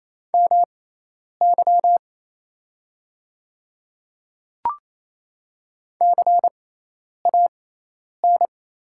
Morse code, heard as a keyed tone, about 27 wpm, spaced as for 10 wpm, 690 Hz. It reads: MY CAN
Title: 6x Character Spacing - Top 100 Words - 28wpm